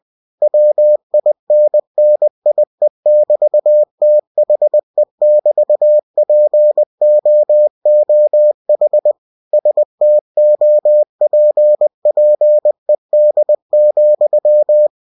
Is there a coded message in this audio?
Yes: Morse code from 0.4 to 14.9 s, heard as a keyed tone, about 20 wpm, 600 Hz. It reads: WINNIE<BT>THE<BT>POOH STOPPED,